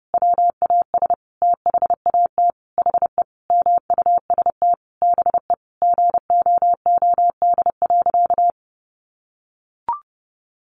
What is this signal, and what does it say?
Morse code, heard as a keyed tone, about 30 wpm, 700 Hz. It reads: WAS THAT HE MUST BE GOOD.
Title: Sentences from Top 1500 Words 30wpm